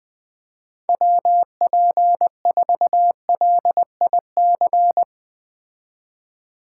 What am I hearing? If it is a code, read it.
Morse code, heard as a keyed tone, about 20 wpm, 700 Hz. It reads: WP4LIC
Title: Speed Racing - USA 2x3 Call Signs 17wpm